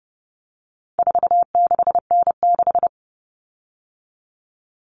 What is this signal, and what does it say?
Morse code, heard as a keyed tone, about 30 wpm, 700 Hz. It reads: 46N6
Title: Common Cut Numbers - 4-Digit 30wpm